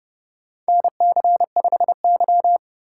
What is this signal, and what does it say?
Morse code, heard as a keyed tone, about 30 wpm, 700 Hz. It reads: NC5Y
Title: USA Call Signs - 4 Characters - 4x Character Spacing - 30wpm